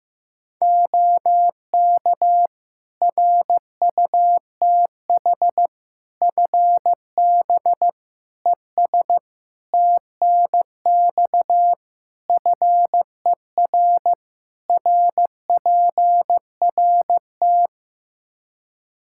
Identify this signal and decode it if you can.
Morse code, heard as a keyed tone, about 16 wpm, 700 Hz. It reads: OK RUTH FB ES TNX FER RPRT
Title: Practice QSO - Exchange 2 - Info - 15wpm